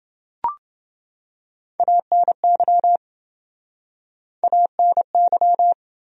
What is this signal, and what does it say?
Morse code, heard as a keyed tone, about 29 wpm, 700 Hz. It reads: ANY ANY